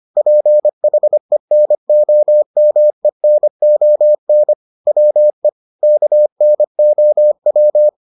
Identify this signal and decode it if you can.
Morse code, heard as a keyed tone, about 25 wpm, 600 Hz. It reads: PHENOMENON WE KNOW